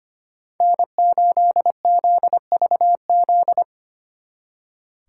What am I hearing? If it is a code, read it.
Morse code, heard as a keyed tone, about 29 wpm, 700 Hz. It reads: N8ZVZ
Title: Speed Racing - USA 1x3 Call Signs 25wpm